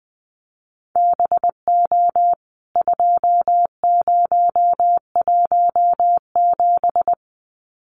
Morse code, heard as a keyed tone, about 20 wpm, 700 Hz. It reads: BO 2017
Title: CWT Exchange 20wpm